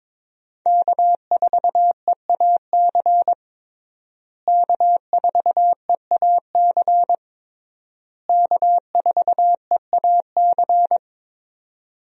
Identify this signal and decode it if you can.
Morse code, heard as a keyed tone, about 22 wpm, 700 Hz. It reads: K4EAC K4EAC K4EAC
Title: USA Call Signs - 5 Characters - Repeated 3x - 22wpm